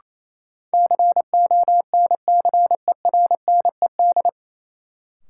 Morse code, heard as a keyed tone, about 28 wpm, 700 Hz. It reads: CONCERNED
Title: Sentences from Top 1000 Words - Review Part 6 - New Word Drill - 28wpm